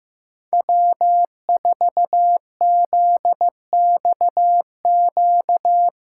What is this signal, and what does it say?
Morse code, heard as a keyed tone, about 15 wpm, 700 Hz. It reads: W4ZXQ
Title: USA Call Signs - 6x Character Spacing - 15wpm